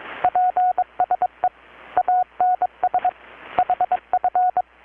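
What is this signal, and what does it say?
Morse code, heard as a keyed tone, about 22 wpm, 720 Hz, with radio noise. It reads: PSE ANS HF
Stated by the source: Morse from KPH's Night of Nights XX maritime radiotelegraphy event, received in Chiapas, Mexico about 3,800 km away on a Kenwood R-600 with a grounded 10 m long-wire antenna.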